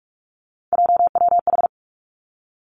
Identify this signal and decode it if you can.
Morse code, heard as a keyed tone, about 46 wpm, 700 Hz. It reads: JWH
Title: Three Challenging Letter-Number Combinations 45wpm